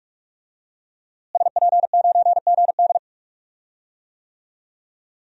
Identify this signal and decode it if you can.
Morse code, heard as a keyed tone, about 46 wpm, 700 Hz. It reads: SP9GD
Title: DXCC-100 Call Signs - 5 Characters - Repeated 3x - 45wpm